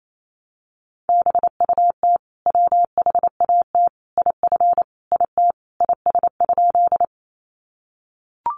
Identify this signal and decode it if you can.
Morse code, heard as a keyed tone, about 28 wpm, 700 Hz. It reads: BUT WHAT IF IT IS?
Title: Sentences from Top 100 Words 28wpm